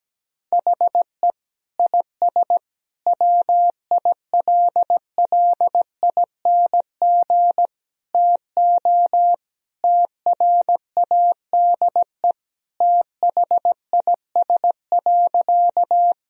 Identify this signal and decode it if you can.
Morse code, heard as a keyed tone, about 17 wpm, 700 Hz. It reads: HE IS WILLING TO TRADE THIS.
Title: Sentences from Top 1000 Words - Review Part 4 - 17wpm